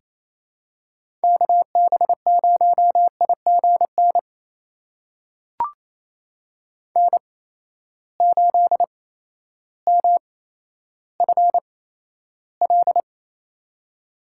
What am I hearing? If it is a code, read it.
Morse code, heard as a keyed tone, about 28 wpm, spaced as for 13 wpm, 700 Hz. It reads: KB0IGN N8MFL
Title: USA Call Signs - 8x Character Spacing - 28wpm